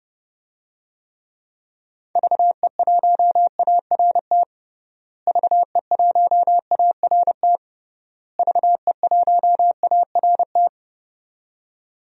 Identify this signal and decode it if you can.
Morse code, heard as a keyed tone, about 30 wpm, 700 Hz. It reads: VE1ART VE1ART VE1ART
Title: DXCC-100 Call Signs - 6 Characters - Repeated 3x - 30wpm